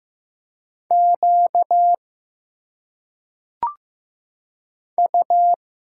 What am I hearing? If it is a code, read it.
Morse code, heard as a keyed tone, about 15 wpm, 700 Hz. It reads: Q U
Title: Single Letters - Phonetic Pronunciation 15wpm